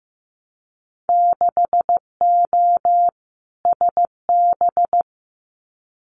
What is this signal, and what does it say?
Morse code, heard as a keyed tone, about 15 wpm, 700 Hz. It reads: 6O SB
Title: Winter Field Day Exchanges Encore 15wpm